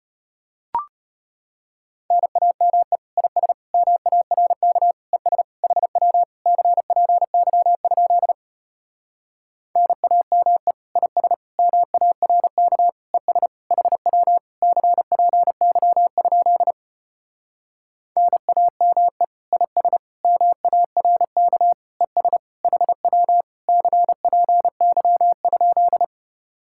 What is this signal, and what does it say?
Morse code, heard as a keyed tone, about 33 wpm, 700 Hz. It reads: NAME IS MARK ES HW CPY? NAME IS MARK ES HW CPY? NAME IS MARK ES HW CPY?